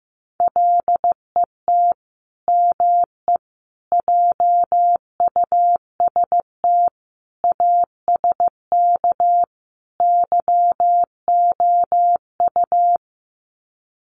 Morse code, heard as a keyed tone, about 15 wpm, 700 Hz. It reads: LET ME JUST ASK YOU
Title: Sets of 5 Words Encore2 15wpm